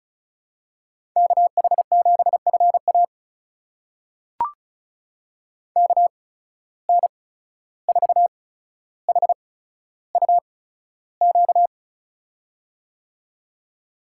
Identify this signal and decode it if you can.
Morse code, heard as a keyed tone, about 36 wpm, spaced as for 14 wpm, 700 Hz. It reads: KH7FA KN4HUQ
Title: USA Call Signs - 8x Character Spacing - 35wpm